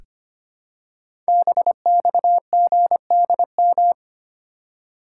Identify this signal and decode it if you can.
Morse code, heard as a keyed tone, about 25 wpm, 700 Hz. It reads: BXGDM
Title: Code Groups - 5 Letters - 25wpm